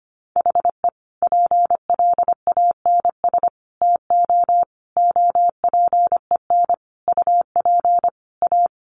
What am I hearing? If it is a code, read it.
Morse code, heard as a keyed tone, about 25 wpm, 700 Hz. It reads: HE PLANS TO OPEN UP A
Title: Additional Easy Phrases 25wpm